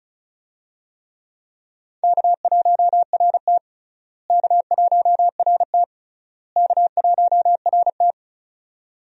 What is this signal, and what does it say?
Morse code, heard as a keyed tone, about 34 wpm, 700 Hz. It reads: K1RT K1RT K1RT
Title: USA Call Signs - 4 Characters - Repeated 3x - 35wpm